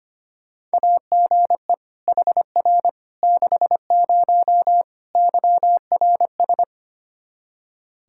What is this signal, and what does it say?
Morse code, heard as a keyed tone, about 25 wpm, 700 Hz. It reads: AGE HR 60 YRS
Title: Practice QSO - Exchange 3 - More Info - 25wpm